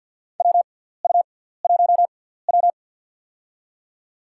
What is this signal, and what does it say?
Morse code, heard as a keyed tone, about 53 wpm, spaced as for 17 wpm, 700 Hz. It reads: WU1W
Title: USA Call Signs - 4 Characters - 6x Character Spacing - 50wpm